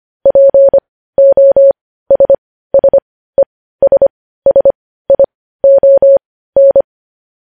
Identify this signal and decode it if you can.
Morse code, heard as a keyed tone, about 25 wpm, spaced as for 15 wpm, 550 Hz. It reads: POSSESSION
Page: news headlines in morse code at 15 wpm